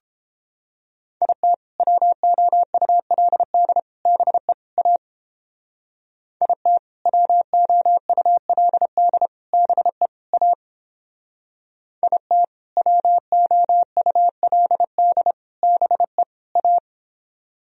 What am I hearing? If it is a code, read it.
Morse code, heard as a keyed tone, about 31 wpm, 700 Hz. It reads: IT WOULD BE A IT WOULD BE A IT WOULD BE A